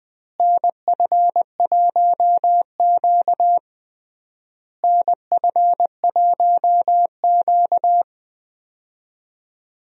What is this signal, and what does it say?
Morse code, heard as a keyed tone, about 20 wpm, 700 Hz. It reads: NF1Q NF1Q
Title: USA Call Signs - 4 Characters - Repeated 3x - 20wpm